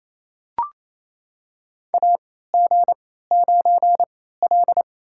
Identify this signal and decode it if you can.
Morse code, heard as a keyed tone, about 28 wpm, spaced as for 16 wpm, 700 Hz. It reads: AG9L
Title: USA Call Signs - 4 Characters - 3x Character Spacing - 28wpm